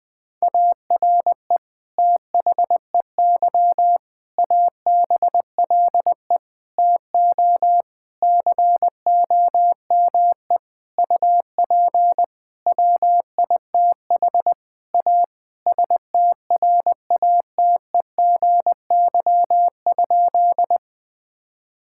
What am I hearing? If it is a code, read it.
Morse code, heard as a keyed tone, about 20 wpm, 700 Hz. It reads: ARE THEY ABLE TO COME UP WITH A STRATEGY?
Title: Sentences from Top 1000 Words - Review Part 5 - 20wpm